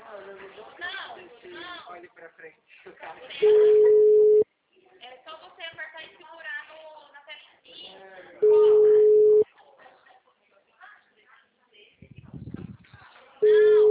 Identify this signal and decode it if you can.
Ringback tone (Europe, 425 Hz), with speech.
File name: out- -2006- - - .34613